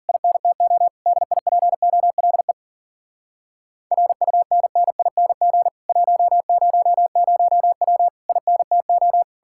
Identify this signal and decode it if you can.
Morse code, heard as a keyed tone, about 52 wpm, 700 Hz. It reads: INTO DIPOLE RUNNING 100W INTO